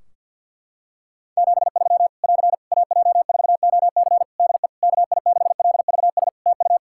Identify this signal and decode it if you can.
Morse code, heard as a keyed tone, about 52 wpm, 700 Hz. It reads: 73 <AR> AJ4OC DE KI6LVS TU
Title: Practice QSO - Exchange 4 - Ending - 50wpm